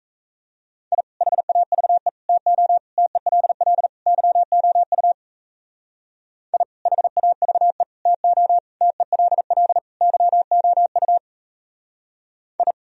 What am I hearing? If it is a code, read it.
Morse code, heard as a keyed tone, about 34 wpm, 700 Hz. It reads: I HAVE TO TELL YOU I HAVE TO TELL YOU I
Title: Speed-Racing - Sets of 5 Words 28wpm